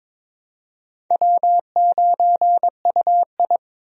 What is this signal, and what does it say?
Morse code, heard as a keyed tone, about 22 wpm, 700 Hz. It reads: W9UI
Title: USA Call Signs - 4 Characters - 22wpm